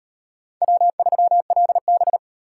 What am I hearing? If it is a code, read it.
Morse code, heard as a keyed tone, about 38 wpm, 700 Hz. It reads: W3LB